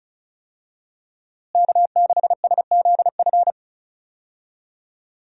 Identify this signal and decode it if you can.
Morse code, heard as a keyed tone, about 36 wpm, 700 Hz. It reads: K6SZF
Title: USA Call Signs - 5 Characters - Repeated 3x - 35wpm